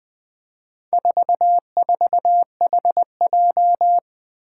Morse code, heard as a keyed tone, about 21 wpm, 700 Hz. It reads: 44HJ
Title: Four Challenging Letter-Number Combinations 20wpm